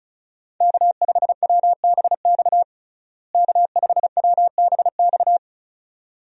Morse code, heard as a keyed tone, about 36 wpm, 700 Hz. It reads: K5WBX K5WBX
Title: USA Call Signs - 5 Characters - Repeated 2x - 35wpm